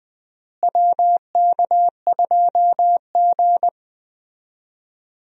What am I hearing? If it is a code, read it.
Morse code, heard as a keyed tone, about 20 wpm, 700 Hz. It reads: WK2G
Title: USA Call Signs - 4 Characters - Repeated 3x - 20wpm